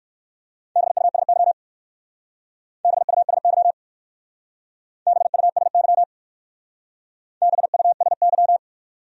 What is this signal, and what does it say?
Morse code, heard as a keyed tone, about 56 wpm, 700 Hz. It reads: BUSY BUSY BUSY BUSY
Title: Speed-Racing Top 500 Words 45wpm